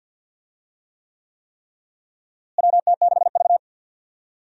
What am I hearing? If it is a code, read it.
Morse code, heard as a keyed tone, about 53 wpm, 700 Hz. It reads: WT6V
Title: USA Call Signs - 4 Characters - Repeated 3x - 50wpm